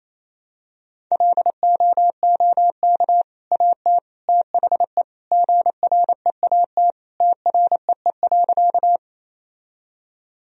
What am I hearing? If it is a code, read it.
Morse code, heard as a keyed tone, about 28 wpm, 700 Hz. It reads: LOOK AT THE GREAT TREE.
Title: Sentences from Top 1000 Words - Review Part 1 - 28wpm